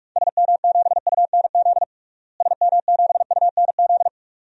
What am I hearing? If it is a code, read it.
Morse code, heard as a keyed tone, about 47 wpm, 700 Hz. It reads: SM7UNZ SM7UNZ
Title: DXCC-100 Call Signs - 6 Characters - Repeated 2x - 45wpm